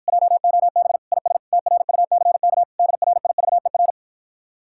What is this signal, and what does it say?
Morse code, heard as a keyed tone, about 53 wpm, 700 Hz. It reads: JOB IS TRUCK DRIVER